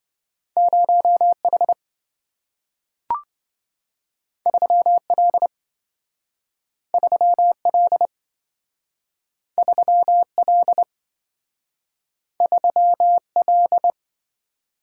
Morse code, heard as a keyed tone, about 27 wpm, 700 Hz. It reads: 0H 3L 3L 3L 3L